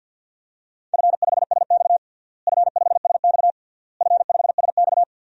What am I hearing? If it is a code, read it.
Morse code, heard as a keyed tone, about 52 wpm, 700 Hz. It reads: F5SX F5SX F5SX
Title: DXCC-100 Call Signs - 4 Characters - Repeated 3x - 50wpm